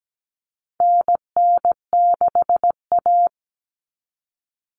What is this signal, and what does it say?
Morse code, heard as a keyed tone, about 17 wpm, 700 Hz. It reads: NN6A
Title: Common Cut Numbers - 4-Digit 17wpm